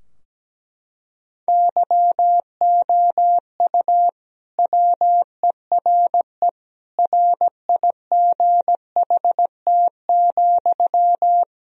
Morse code, heard as a keyed tone, about 17 wpm, 700 Hz. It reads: YOU WERE RIGHT,
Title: Sentences from Top 1000 Words - Review Part 2 - 17wpm